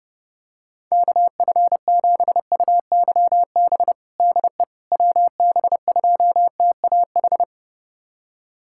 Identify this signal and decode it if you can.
Morse code, heard as a keyed tone, about 32 wpm, 700 Hz. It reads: KF7UYB DE WB2TAH
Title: USA Callsign DE Callsign - 6 Characters - 30wpm